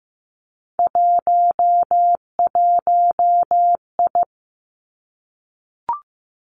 Morse code, heard as a keyed tone, about 15 wpm, 700 Hz. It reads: 11I
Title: Three Challenging Letter-Number Combinations 15wpm